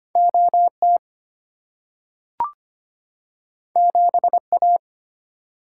Morse code, heard as a keyed tone, about 25 wpm, 700 Hz. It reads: OT 7A
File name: Two Letter-Number Combinations - 25wpm